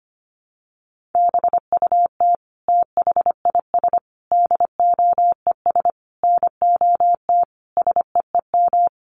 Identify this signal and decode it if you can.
Morse code, heard as a keyed tone, about 25 wpm, 700 Hz. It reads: BUT THIS DOES NOT SEEM
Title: Sentences from Top 1400 Words 25wpm